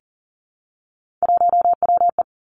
Morse code, heard as a keyed tone, about 38 wpm, 700 Hz. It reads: 1WE
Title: Three Challenging Letter-Number Combinations 40wpm